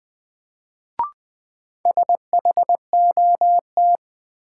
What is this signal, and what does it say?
Morse code, heard as a keyed tone, about 20 wpm, 700 Hz. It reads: SHOT
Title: Sentences from Top 1100 Words - New Word Drill Alt - 20wpm